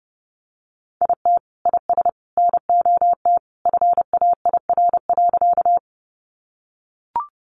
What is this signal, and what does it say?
Morse code, heard as a keyed tone, about 30 wpm, 700 Hz. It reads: IT IS NOT FAIR.
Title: Sentences from Top 1300 Words 30wpm